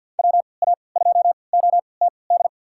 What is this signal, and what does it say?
Morse code, heard as a keyed tone, about 52 wpm, spaced as for 28 wpm, 690 Hz. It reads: WA2OTD
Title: USA Call Signs - 6 Characters - 3x Character Spacing - 50wpm